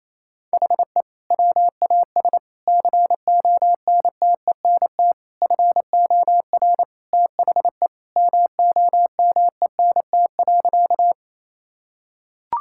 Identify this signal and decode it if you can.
Morse code, heard as a keyed tone, about 28 wpm, 700 Hz. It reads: HE WAS CONTENT FOR THE MOMENT.
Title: Sentences from Top 1000 Words - Review Part 6 - 28wpm